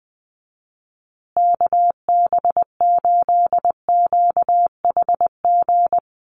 Morse code, heard as a keyed tone, about 20 wpm, 700 Hz. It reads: KB8QHG
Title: USA Call Signs Repeated 2x 20wpm